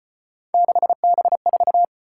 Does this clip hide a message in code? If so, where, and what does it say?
Yes: Morse code from 0.5 to 1.9 s, heard as a keyed tone, about 34 wpm, 700 Hz. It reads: 6B4